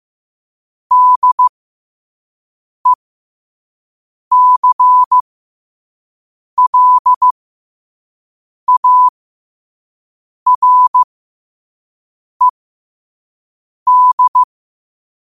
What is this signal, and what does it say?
Morse code, heard as a keyed tone, about 15 wpm, spaced as for 5 wpm, 1000 Hz. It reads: DECLARED
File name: qotdfiles / 5wpm